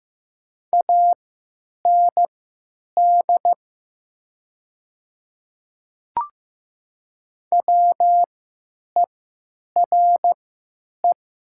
Morse code, heard as a keyed tone, about 15 wpm, spaced as for 9 wpm, 700 Hz. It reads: AND WERE
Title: 3x Character Spacing - Top 100 Words - 15wpm